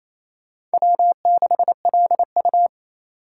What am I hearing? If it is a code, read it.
Morse code, heard as a keyed tone, about 37 wpm, 700 Hz. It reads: W6LU